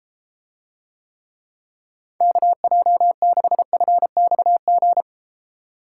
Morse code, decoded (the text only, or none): KJ6FXG